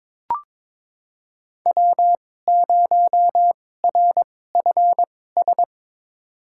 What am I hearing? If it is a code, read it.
Morse code, heard as a keyed tone, about 22 wpm, spaced as for 16 wpm, 700 Hz. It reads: W0RFS
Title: USA Call Signs - 5 Characters - 2x Character Spacing - 22wpm